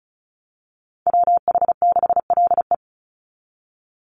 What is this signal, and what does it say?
Morse code, heard as a keyed tone, about 34 wpm, 700 Hz. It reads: WH6LE
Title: NA SOTA Activator Callsigns 35wpm